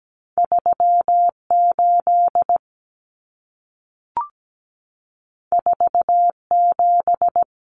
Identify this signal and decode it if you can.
Morse code, heard as a keyed tone, about 17 wpm, 700 Hz. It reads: 38 47
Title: Numbers - 2-Digit 17wpm